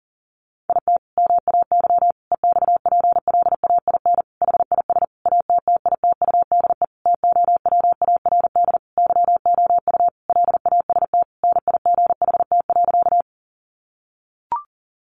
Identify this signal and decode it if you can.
Morse code, heard as a keyed tone, about 40 wpm, 700 Hz. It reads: IT MAY EXPLAIN HIS ATTITUDE TOWARD YOU LAST NIGHT.
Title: Sentences from Top 1300 Words 40wpm